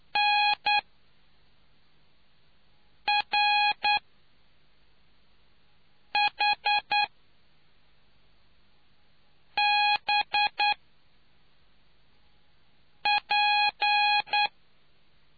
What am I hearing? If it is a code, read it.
Morse code, heard as a keyed tone, about 9 wpm, spaced as for 3 wpm, 830 Hz. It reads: NRHBP